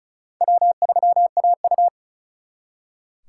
Morse code, heard as a keyed tone, about 43 wpm, 700 Hz. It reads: W3AU
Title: Speed Racing - USA 1x2 Call Signs 35wpm